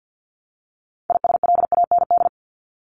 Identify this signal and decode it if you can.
Morse code, heard as a keyed tone, about 48 wpm, 700 Hz. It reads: ISLAND
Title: Top 500 Words Repeated 50wpm